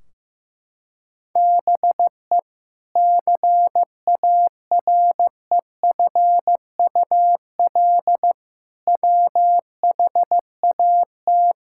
Morse code, heard as a keyed tone, about 15 wpm, 700 Hz. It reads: BE CAREFUL WHAT
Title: Sentences from Top 1600 Words 15wpm